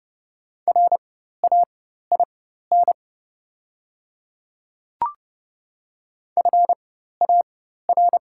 Morse code, heard as a keyed tone, about 30 wpm, spaced as for 14 wpm, 700 Hz. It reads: RAIN FAR
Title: 4x Character Spacing - Top 300 Words - 30wpm